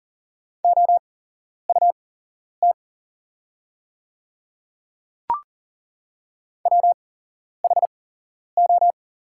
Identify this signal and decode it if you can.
Morse code, heard as a keyed tone, about 40 wpm, spaced as for 11 wpm, 700 Hz. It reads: OUT WHO
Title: 8x Character Spacing - Top 100 Words - 40wpm